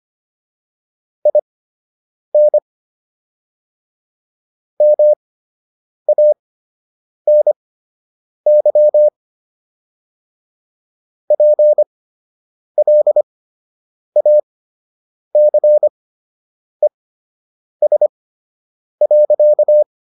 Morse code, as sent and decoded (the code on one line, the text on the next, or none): .. -. / -- .- -. -.-- / .--. .-.. .- -.-. ... .-.-.-
IN MANY PLACES.